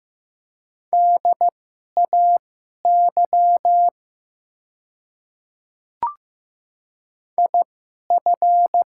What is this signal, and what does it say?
Morse code, heard as a keyed tone, about 15 wpm, spaced as for 11 wpm, 700 Hz. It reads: DAY IF
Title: 2x Character Spacing - Top 100 Words - 15wpm